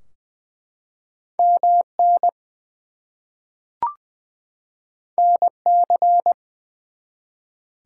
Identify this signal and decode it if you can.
Morse code, heard as a keyed tone, about 20 wpm, 700 Hz. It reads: MN NC
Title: US States and Canadian Provinces 20wpm